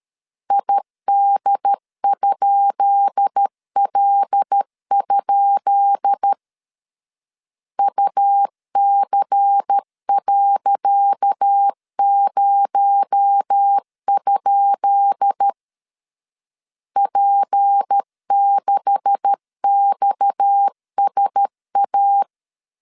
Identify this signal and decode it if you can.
Morse code, heard as a keyed tone, about 13 wpm, 800 Hz. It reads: ID?L? UC.0? P6XSA
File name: LEZIONI / 10 SIGNS / 10C DOT 12wpm 2451